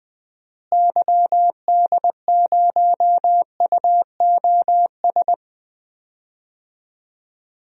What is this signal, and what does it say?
Morse code, heard as a keyed tone, about 21 wpm, 700 Hz. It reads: YD0UOS
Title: DXCC-100 Call Signs - 6 Characters - Repeated 2x - 20wpm